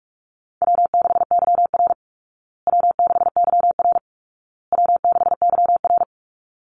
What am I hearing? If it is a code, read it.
Morse code, heard as a keyed tone, about 44 wpm, 700 Hz. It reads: W6YR W6YR W6YR
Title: Easy Call Signs Repeated 3x - 45wpm